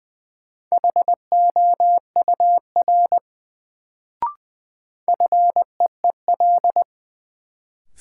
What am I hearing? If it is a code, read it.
Morse code, heard as a keyed tone, about 21 wpm, 700 Hz. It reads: HOUR FEEL
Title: Sets of 5 Words - New Word Drill - 20wpm